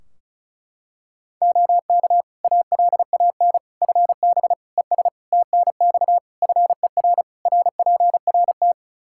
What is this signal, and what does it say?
Morse code, heard as a keyed tone, about 34 wpm, 700 Hz. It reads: OK ALAN FB ES TNX FER RPRT
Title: Practice QSO - Exchange 2 - Info - 35wpm